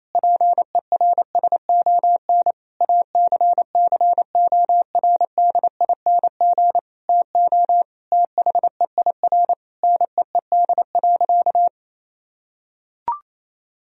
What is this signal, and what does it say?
Morse code, heard as a keyed tone, about 28 wpm, 700 Hz. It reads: PERSON ACCORDING TO THEIR NEED.
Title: Sentences from Top 1000 Words - Review Part 4 - 28wpm